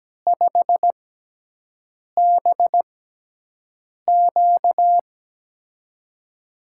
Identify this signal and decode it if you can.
Morse code, heard as a keyed tone, about 17 wpm, spaced as for 6 wpm, 700 Hz. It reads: 5BQ
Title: USA Call Signs - 5 Characters - 6x Character Spacing - 17wpm